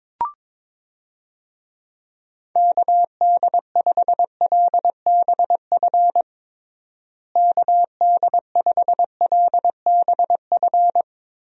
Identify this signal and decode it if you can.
Morse code, heard as a keyed tone, about 22 wpm, 700 Hz. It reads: KD5LBF KD5LBF
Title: USA Call Signs - 6 Characters - Repeated 3x - 22wpm